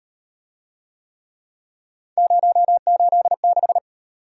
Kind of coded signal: Morse code

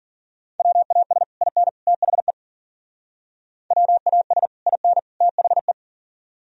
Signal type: Morse code

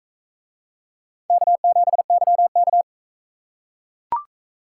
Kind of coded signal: Morse code

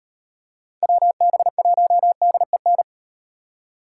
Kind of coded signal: Morse code

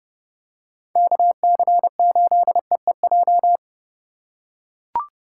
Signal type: Morse code

